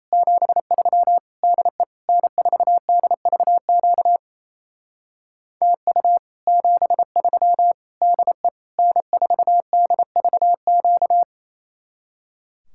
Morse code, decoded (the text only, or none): 73 DE N4DVQ TU 73 DE N4DVQ